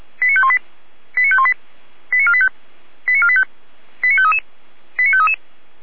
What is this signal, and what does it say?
Five-tone selective call (ZVEI), heard as quick runs of five tones, tones of about 70 ms. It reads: 89518 89518 89485 89485 89533 89533
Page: /455/SDR/ZVEI/89417/